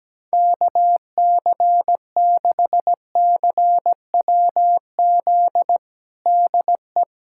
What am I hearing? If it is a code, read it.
Morse code, heard as a keyed tone, about 17 wpm, 700 Hz. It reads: KC6CWZ DE
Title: USA Callsign DE Callsign 17wpm